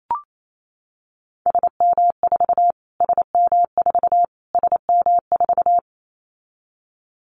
Morse code, heard as a keyed tone, about 28 wpm, 700 Hz. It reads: SM4 SM4 SM4
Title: Callsign Prefixes for Top 25 DXCC - Repeated 3x - 28wpm